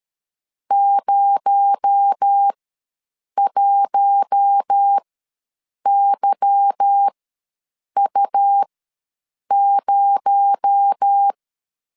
Morse code, heard as a keyed tone, about 13 wpm, spaced as for 7 wpm, 800 Hz. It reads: 01YU0
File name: LEZIONI / 9 28 ws 12wpm 3354